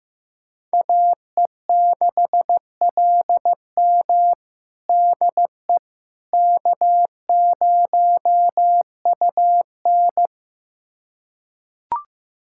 Morse code, heard as a keyed tone, about 15 wpm, 700 Hz. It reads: AE6LM DE K0UN
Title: USA Callsign DE Callsign 15wpm